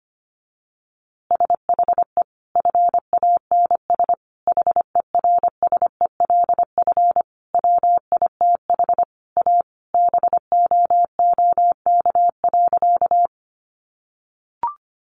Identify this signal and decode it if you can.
Morse code, heard as a keyed tone, about 25 wpm, 700 Hz. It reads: SHE FANS HERSELF WITH A BOOK.
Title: Easy Phrases 25wpm